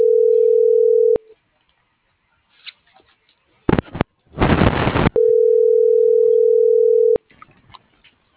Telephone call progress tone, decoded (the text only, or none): ringback tone